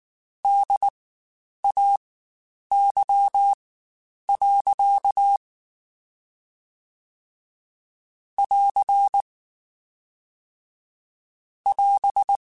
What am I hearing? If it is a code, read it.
Morse code, heard as a keyed tone, about 19 wpm, spaced as for 9 wpm, 800 Hz. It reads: DAY. <AR> <AS>
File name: Spaced Characters / Spaced Characters-mp3s / Weather5 12 sp @18